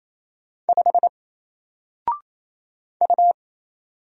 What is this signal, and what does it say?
Morse code, heard as a keyed tone, about 27 wpm, 700 Hz. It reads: 5 U
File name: Single-Letter-Number-28wpm